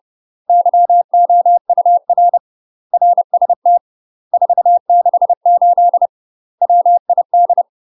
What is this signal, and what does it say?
Morse code, heard as a keyed tone, about 30 wpm, 700 Hz. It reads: YOUR RST 468 WID
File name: UK MASTER QSOs TRAINING / 30WPM / UK MASTER QSOs 0022